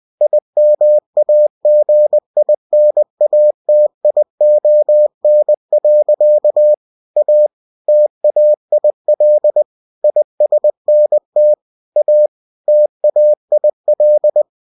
Morse code, heard as a keyed tone, about 20 wpm, 600 Hz. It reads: IMAGINATION. A TAIL ISNT A TAIL
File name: cw-books / winnie-the-pooh / winnie-20wpm-page-0088